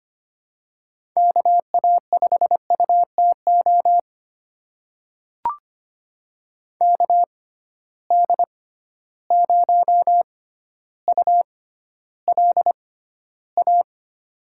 Morse code, heard as a keyed tone, about 25 wpm, spaced as for 13 wpm, 700 Hz. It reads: KA5UTO KD0ULA
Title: USA Call Signs - 6 Characters - 6x Character Spacing - 25wpm